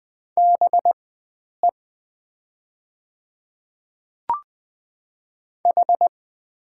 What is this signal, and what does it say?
Morse code, heard as a keyed tone, about 20 wpm, spaced as for 9 wpm, 700 Hz. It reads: BE H
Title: 4x Character Spacing - Top 100 Words - 20wpm